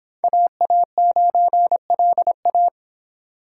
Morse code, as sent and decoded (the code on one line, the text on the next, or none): .- .- ----. .-.. .-
AA9LA